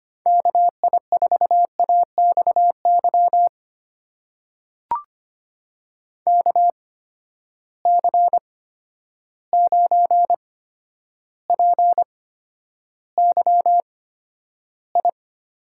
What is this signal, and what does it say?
Morse code, heard as a keyed tone, about 25 wpm, spaced as for 11 wpm, 700 Hz. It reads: KI4AXY KC9PYI
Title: USA Call Signs - 6 Characters - 8x Character Spacing - 25wpm